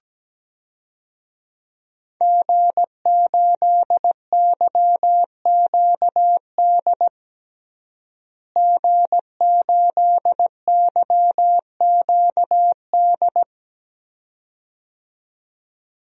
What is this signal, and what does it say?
Morse code, heard as a keyed tone, about 17 wpm, 700 Hz. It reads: G8YQD G8YQD
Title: DXCC-100 Call Signs - 5 Characters - Repeated 2x - 17wpm